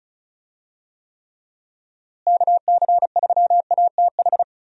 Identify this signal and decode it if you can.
Morse code, heard as a keyed tone, about 36 wpm, 700 Hz. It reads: KC3ATH
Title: USA Call Signs - 6 Characters - Repeated 2x - 35wpm